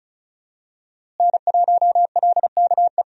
Morse code, heard as a keyed tone, about 34 wpm, 700 Hz. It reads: N1LKE
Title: USA Call Signs - 5 Characters - Repeated 2x - 35wpm